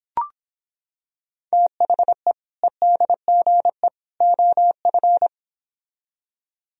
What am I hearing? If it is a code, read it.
Morse code, heard as a keyed tone, about 26 wpm, 700 Hz. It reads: THE EDGE OF